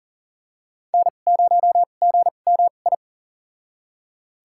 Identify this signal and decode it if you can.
Morse code, heard as a keyed tone, about 40 wpm, spaced as for 29 wpm, 700 Hz. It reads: N0GMI
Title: USA Call Signs - 5 Characters - 2x Character Spacing - 40wpm